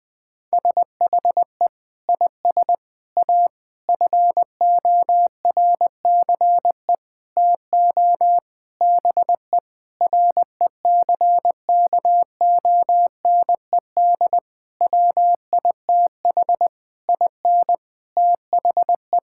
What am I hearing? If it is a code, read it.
Morse code, heard as a keyed tone, about 20 wpm, 700 Hz. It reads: SHE IS A FORCE TO BE RECKONED WITH IN THE
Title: Sentences with 8-Word Phrases - 20wpm